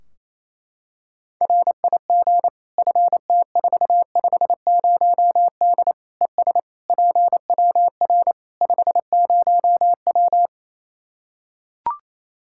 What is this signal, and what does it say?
Morse code, heard as a keyed tone, about 28 wpm, 700 Hz. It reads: RIG FT450D ES PWR 50W
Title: K1USN SST QSO Practice 28wpm